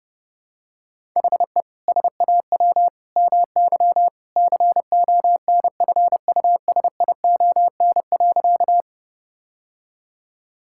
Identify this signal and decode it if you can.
Morse code, heard as a keyed tone, about 32 wpm, 700 Hz. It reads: HE SAW MY CONFUSION.
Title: Sentences from Top 1600 Words 30wpm